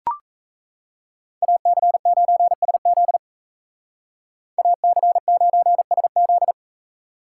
Morse code, heard as a keyed tone, about 35 wpm, 700 Hz. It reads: AC9SZ AC9SZ